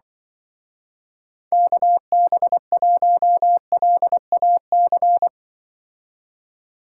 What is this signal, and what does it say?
Morse code, heard as a keyed tone, about 24 wpm, 700 Hz. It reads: KB1LAC
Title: Speed Racing - USA 2x3 Call Signs 20wpm